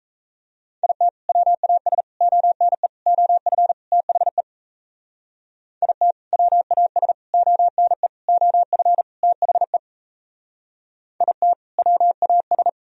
Morse code, heard as a keyed tone, about 35 wpm, 700 Hz. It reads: IT WAS ONE OF THE IT WAS ONE OF THE IT WAS